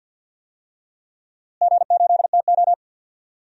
Morse code, heard as a keyed tone, about 51 wpm, 700 Hz. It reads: G8TO
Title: DXCC-100 Call Signs - 4 Characters - Repeated 3x - 50wpm